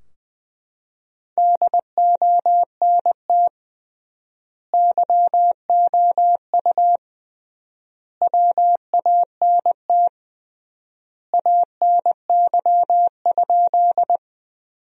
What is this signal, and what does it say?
Morse code, heard as a keyed tone, about 20 wpm, 700 Hz. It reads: DONT YOU WANT ANY?
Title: 3x Word-Spacing - Sentences from Top 300 Words - 20wpm